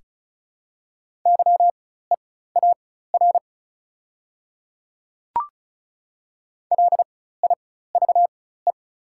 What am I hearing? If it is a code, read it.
Morse code, heard as a keyed tone, about 34 wpm, spaced as for 16 wpm, 700 Hz. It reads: YEAR LIVE
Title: 4x Character Spacing - Top 200 Words - 35wpm